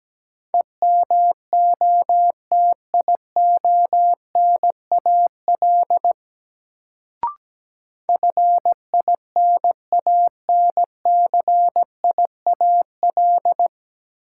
Morse code, heard as a keyed tone, about 17 wpm, 700 Hz. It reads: EMOTIONAL FINANCIAL
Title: Sentences from Top 1000 Words - Review Part 5 - New Word Drill - 17wpm